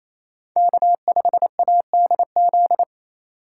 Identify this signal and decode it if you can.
Morse code, heard as a keyed tone, about 28 wpm, 700 Hz. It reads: K5ADZ
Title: USA Call Signs - 5 Characters - 4x Character Spacing - 28wpm